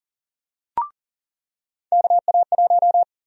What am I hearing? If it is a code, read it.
Morse code, heard as a keyed tone, about 40 wpm, 700 Hz. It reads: KA1
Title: USA Call Sign Prefixes - 40wpm